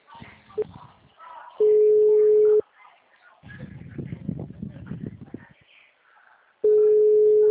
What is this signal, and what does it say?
Ringback tone (Europe, 425 Hz), with speech.